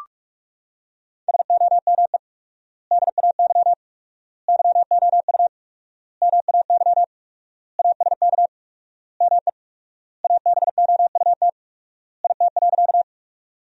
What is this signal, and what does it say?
Morse code, heard as a keyed tone, about 45 wpm, 700 Hz. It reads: SOME DAY YOU MAY ASK ME ABOUT IT.
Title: 4x Word-Spacing - Sentences from Top 200 Words - 45wpm